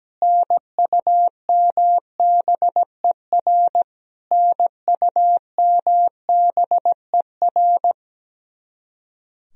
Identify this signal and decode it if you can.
Morse code, heard as a keyed tone, about 17 wpm, 700 Hz. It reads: NUMBER NUMBER
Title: Top 100 Words Repeated 17wpm